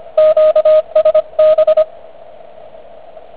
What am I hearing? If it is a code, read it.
Morse code, heard as a keyed tone, about 25 wpm, 620 Hz, with radio noise. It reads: QSB